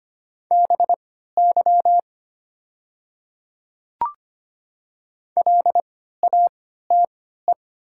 Morse code, heard as a keyed tone, about 25 wpm, spaced as for 14 wpm, 700 Hz. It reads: BY LATE
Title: 3x Character Spacing - Top 300 Words - 25wpm